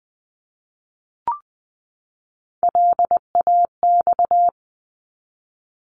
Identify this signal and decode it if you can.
Morse code, heard as a keyed tone, about 20 wpm, 700 Hz. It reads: LAX
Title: ARRL Sections 20wpm